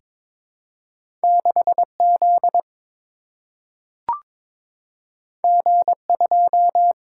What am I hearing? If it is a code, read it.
Morse code, heard as a keyed tone, about 22 wpm, 700 Hz. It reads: 6Z G2
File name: practice / 22wpm / Two Letter-Number Combinations - 22wpm